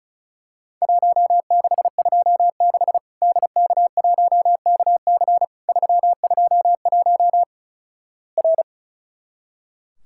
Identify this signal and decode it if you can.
Morse code, heard as a keyed tone, about 36 wpm, 700 Hz. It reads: 1626 DK1KC 321 R
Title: Worked All Europe DX Contest - QTC Exchange - 35wpm